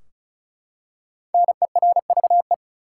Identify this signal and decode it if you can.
Morse code, heard as a keyed tone, about 34 wpm, 700 Hz. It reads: NERVE
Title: Five Letter Words - 35wpm